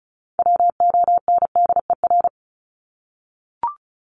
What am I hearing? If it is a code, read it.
Morse code, heard as a keyed tone, about 34 wpm, 700 Hz. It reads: WONDER
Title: Top 500 Words Repeated 35wpm